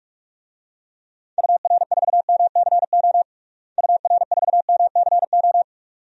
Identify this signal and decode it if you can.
Morse code, heard as a keyed tone, about 43 wpm, 700 Hz. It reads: UR4MCO UR4MCO
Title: DXCC-100 Call Signs - 6 Characters - Repeated 2x - 45wpm